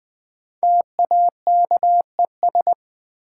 Morse code, heard as a keyed tone, about 20 wpm, 700 Hz. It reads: TAKES
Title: Sentences from Top 1000 Words - Review Part 6 - New Word Drill - 20wpm